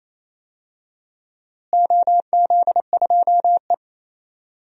Morse code, heard as a keyed tone, about 28 wpm, 700 Hz. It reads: OZ2E
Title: DXCC-100 Call Signs - 4 Characters - Repeated 2x - 28wpm